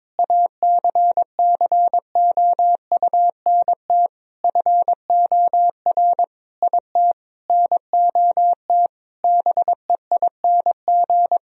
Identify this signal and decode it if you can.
Morse code, heard as a keyed tone, about 22 wpm, 700 Hz. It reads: ACCOUNT FOR IT NOT BEING
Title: Sentences from Top 1000 Words - Review Part 3 - 22wpm